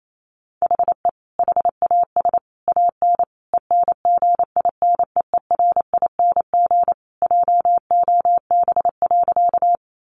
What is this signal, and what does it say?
Morse code, heard as a keyed tone, about 28 wpm, 700 Hz. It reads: HE HAS AN ENGINEERING JOB.
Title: Sentences from Top 1500 Words 28wpm